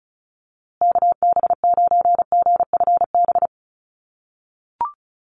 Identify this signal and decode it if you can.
Morse code, heard as a keyed tone, about 34 wpm, 700 Hz. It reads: KB9GFB
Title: USA Call Signs Repeated 2x 35wpm